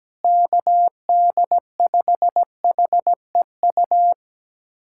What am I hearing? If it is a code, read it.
Morse code, heard as a keyed tone, about 17 wpm, 700 Hz. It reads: KD5HEU
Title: USA Callsign DE Callsign - 6 Characters - 17wpm